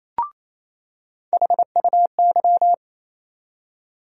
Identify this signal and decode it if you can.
Morse code, heard as a keyed tone, about 28 wpm, 700 Hz. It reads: HUY
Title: USA Call Sign Postfixes - 3 Letters 28wpm